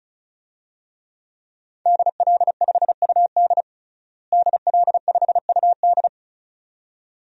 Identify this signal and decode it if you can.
Morse code, heard as a keyed tone, about 34 wpm, 700 Hz. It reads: DL5UD DL5UD
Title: DXCC-100 Call Signs - 5 Characters - Repeated 2x - 35wpm